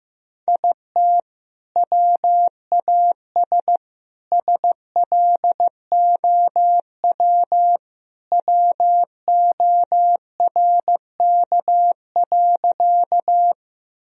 Morse code, heard as a keyed tone, about 15 wpm, 700 Hz. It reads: IT WAS SLOW WORK.
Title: Sentences from Top 1000 Words - Review Part 2 - 15wpm